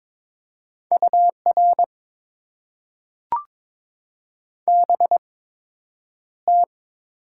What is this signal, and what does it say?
Morse code, heard as a keyed tone, about 22 wpm, spaced as for 9 wpm, 700 Hz. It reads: UR BT
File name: Top 90 QSO Elements - 8x Character Spacing - 22wpm